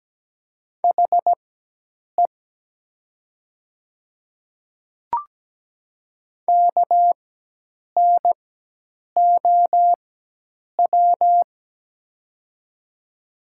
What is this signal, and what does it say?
Morse code, heard as a keyed tone, about 17 wpm, spaced as for 8 wpm, 700 Hz. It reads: HE KNOW
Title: 4x Character Spacing - Top 200 Words - 17wpm